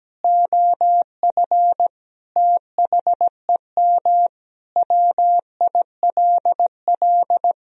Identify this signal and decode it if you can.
Morse code, heard as a keyed tone, about 17 wpm, 700 Hz. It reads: OF THEM WILL
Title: Sentences from Top 1000 Words - Review Part 3 - 17wpm